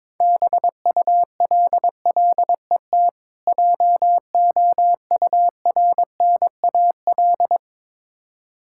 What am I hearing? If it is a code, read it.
Morse code, heard as a keyed tone, about 22 wpm, 690 Hz. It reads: BULLET JOURNAL
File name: Difficult Words with Hints - 22wpm